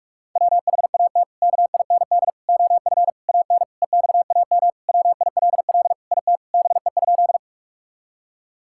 Morse code, heard as a keyed tone, about 46 wpm, 700 Hz. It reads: WHAT KIND OF AN EXAM WILL IT BE?
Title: Sentences from Top 1000 Words - Review Part 5 - 45wpm